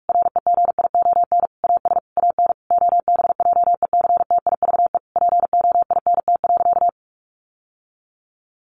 Morse code, heard as a keyed tone, about 44 wpm, 700 Hz. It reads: REGION AS AN OBJECTIVE POINT.
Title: Sentences from Top 1500 Words 45wpm